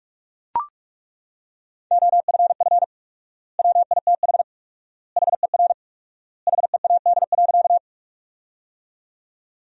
Morse code, heard as a keyed tone, about 41 wpm, 700 Hz. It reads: OFF WITH HER HEAD.
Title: 4x Word-Spacing - Sentences from Top 300 Words - 45wpm